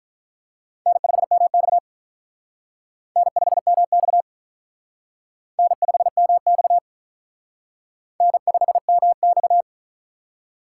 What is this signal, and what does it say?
Morse code, heard as a keyed tone, about 43 wpm, 700 Hz. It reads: N5MX N5MX N5MX N5MX
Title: Speed Racing - USA 1x2 Call Signs 35wpm